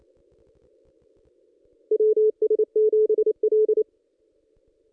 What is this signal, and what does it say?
Morse code, heard as a keyed tone, about 29 wpm, 430 Hz. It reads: WS7L
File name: contests / ss-cw-k5tr-op-n5rz -1900